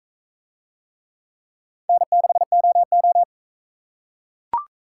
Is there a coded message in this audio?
Yes: Morse code from 1.9 to 3.2 s, heard as a keyed tone, about 44 wpm, 700 Hz. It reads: N6OO